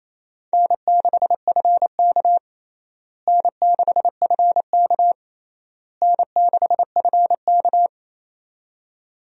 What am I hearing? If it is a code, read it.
Morse code, heard as a keyed tone, about 28 wpm, 700 Hz. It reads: N6FK N6FK N6FK
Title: USA Call Signs - 4 Characters - Repeated 3x - 28wpm